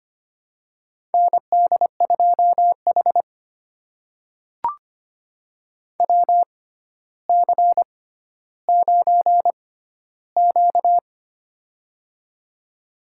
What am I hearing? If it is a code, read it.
Morse code, heard as a keyed tone, about 25 wpm, spaced as for 13 wpm, 700 Hz. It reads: ND2H WC9Q
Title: USA Call Signs - 4 Characters - 6x Character Spacing - 25wpm